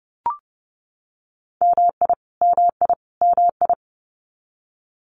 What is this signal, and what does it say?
Morse code, heard as a keyed tone, about 30 wpm, 700 Hz. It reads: MI MI MI
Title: US States Repeated 3x 30wpm